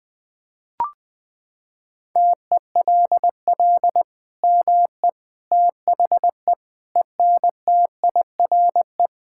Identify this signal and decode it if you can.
Morse code, heard as a keyed tone, about 20 wpm, 700 Hz. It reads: TELL ME THE ENTIRE
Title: Sentences from Top 1000 Words - Review Part 5 - 20wpm